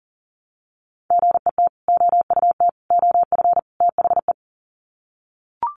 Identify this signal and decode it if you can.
Morse code, heard as a keyed tone, about 40 wpm, 700 Hz. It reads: GET OUT OF THE
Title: Sets of 4 Words 40wpm